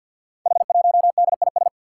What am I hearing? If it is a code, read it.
Morse code, heard as a keyed tone, about 52 wpm, 700 Hz. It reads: H1DIS
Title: Five Challenging Letter-Number Combinations 50wpm